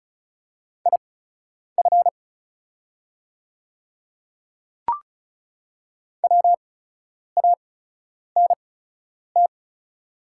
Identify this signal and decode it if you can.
Morse code, heard as a keyed tone, about 36 wpm, spaced as for 10 wpm, 700 Hz. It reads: IF WANT
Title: 8x Character Spacing - Top 300 Words - 35wpm